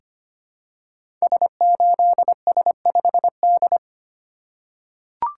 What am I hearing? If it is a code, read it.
Morse code, heard as a keyed tone, about 25 wpm, 700 Hz. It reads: S8S5D
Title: Code Groups - 5 Letter:Numbers - 25wpm